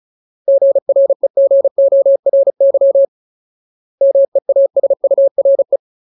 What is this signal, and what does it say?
Morse code, heard as a keyed tone, about 34 wpm, 550 Hz. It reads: GREGORY MEASURE
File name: Ultimate Word List / 35WPM Ultimate